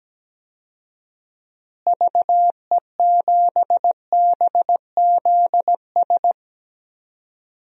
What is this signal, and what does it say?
Morse code, heard as a keyed tone, about 17 wpm, 700 Hz. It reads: VE7BZS
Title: DXCC-100 Call Signs - 6 Characters - Repeated 2x - 17wpm